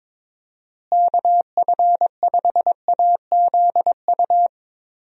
Morse code, heard as a keyed tone, about 22 wpm, 700 Hz. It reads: KF5AZU
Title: USA Call Signs - 6 Characters - Repeated 2x - 22wpm